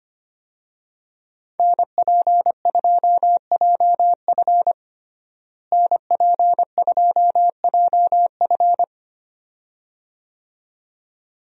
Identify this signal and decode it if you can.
Morse code, heard as a keyed tone, about 25 wpm, 700 Hz. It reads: NP2JF NP2JF
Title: USA Call Signs - 5 Characters - Repeated 2x - 25wpm